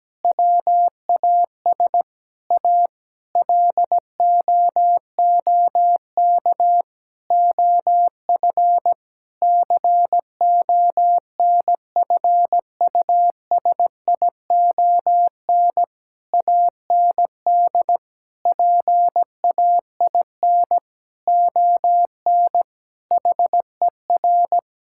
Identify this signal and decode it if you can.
Morse code, heard as a keyed tone, about 17 wpm, 700 Hz. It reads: WAS A LOOK OF CONFUSION AND PAIN ON HER
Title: Sentences from Top 1600 Words 17wpm